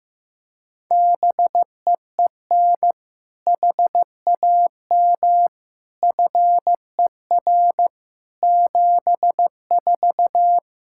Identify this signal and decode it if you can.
Morse code, heard as a keyed tone, about 15 wpm, 700 Hz. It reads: BEEN HAM FER 74
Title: Practice QSO - Exchange 3 - More Info - 15wpm